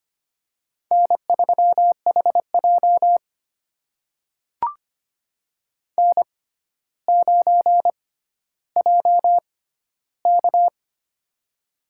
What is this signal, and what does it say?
Morse code, heard as a keyed tone, about 25 wpm, spaced as for 13 wpm, 700 Hz. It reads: N3HJ N9JK
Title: USA Call Signs - 4 Characters - 6x Character Spacing - 25wpm